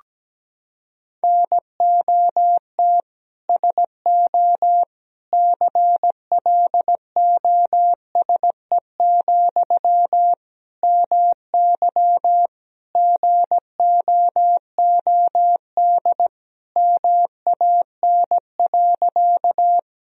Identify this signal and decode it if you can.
Morse code, heard as a keyed tone, about 17 wpm, 700 Hz. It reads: NOT SO CLOSE, MY GOOD MAN.
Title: Sentences from Top 1000 Words - Review Part 1 - 17wpm